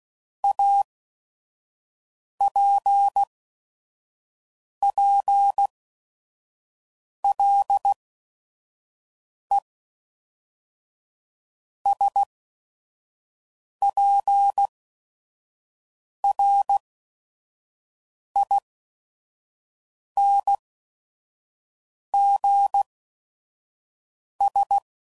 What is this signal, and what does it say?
Morse code, heard as a keyed tone, about 16 wpm, spaced as for 5 wpm, 800 Hz. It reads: APPLESPRINGS
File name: CW Code Groups @5WPM / fromSM's / Cities